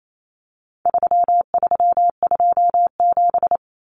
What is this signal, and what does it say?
Morse code, heard as a keyed tone, about 28 wpm, 700 Hz. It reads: 3327
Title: Common Cut Numbers - 4-Digit 28wpm